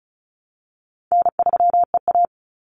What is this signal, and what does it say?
Morse code, heard as a keyed tone, about 34 wpm, 700 Hz. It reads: N3EA
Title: Easy Call Signs Repeated 2x - 35wpm